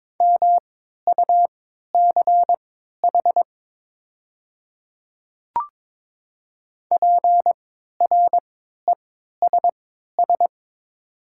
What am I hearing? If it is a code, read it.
Morse code, heard as a keyed tone, about 22 wpm, spaced as for 12 wpm, 700 Hz. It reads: MUCH PRESS
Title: 3x Character Spacing - Top 300 Words - 22wpm